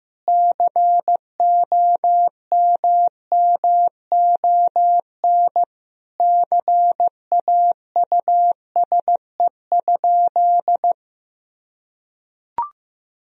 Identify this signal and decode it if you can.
Morse code, heard as a keyed tone, about 15 wpm, 700 Hz. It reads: COMMON CAUSE?
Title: Sentences from Top 1000 Words - Review Part 3 - 15wpm